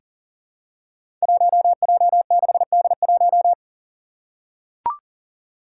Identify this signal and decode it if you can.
Morse code, heard as a keyed tone, about 38 wpm, 700 Hz. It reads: 1J6D1
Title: Five Challenging Letter-Number Combinations 40wpm